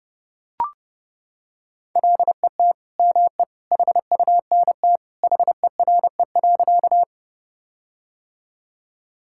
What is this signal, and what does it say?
Morse code, heard as a keyed tone, about 32 wpm, 700 Hz. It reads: LET ME HUNT HERE.
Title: Sentences from Top 1600 Words 30wpm